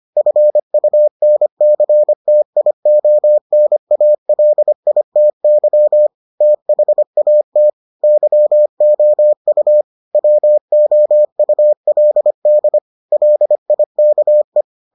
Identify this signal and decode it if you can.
Morse code, heard as a keyed tone, about 25 wpm, 600 Hz. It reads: FUNCTIONALITY THAT YOU WOULD LIKE